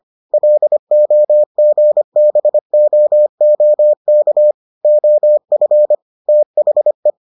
Morse code, heard as a keyed tone, about 25 wpm, 600 Hz. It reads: LOGBOOK OF THE